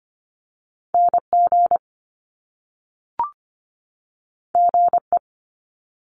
Morse code, heard as a keyed tone, about 25 wpm, 700 Hz. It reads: NG GE
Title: Top Two Letter Combinations in Words 25wpm